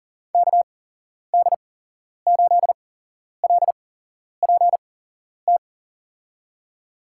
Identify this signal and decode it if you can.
Morse code, heard as a keyed tone, about 40 wpm, spaced as for 11 wpm, 700 Hz. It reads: KD8LPT
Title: USA Call Signs - 8x Character Spacing - 40wpm